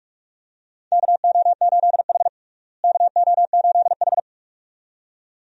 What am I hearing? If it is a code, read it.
Morse code, heard as a keyed tone, about 46 wpm, 700 Hz. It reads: KO8H KO8H
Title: USA Call Signs - 4 Characters - Repeated 2x - 45wpm